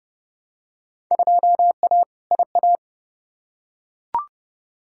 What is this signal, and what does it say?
Morse code, heard as a keyed tone, about 30 wpm, 700 Hz. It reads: 2A IA
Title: Field Day Exchanges Encore - 30wpm